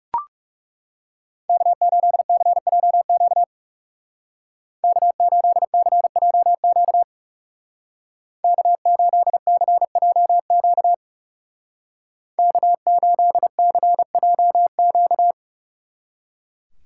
Morse code, heard as a keyed tone, about 35 wpm, 700 Hz. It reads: K8CJQ K8CJQ K8CJQ K8CJQ
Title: Speed Racing - USA Call Signs 30wpm